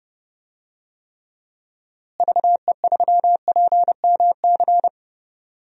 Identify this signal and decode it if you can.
Morse code, heard as a keyed tone, about 30 wpm, 700 Hz. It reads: VE3PMC
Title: DXCC-100 Call Signs - 6 Characters - Repeated 3x - 30wpm